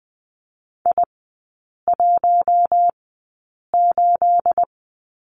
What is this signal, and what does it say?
Morse code, heard as a keyed tone, about 20 wpm, spaced as for 8 wpm, 690 Hz. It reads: I18
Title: 212 Lesson 34e - Number 8 - Three Characters 20wpm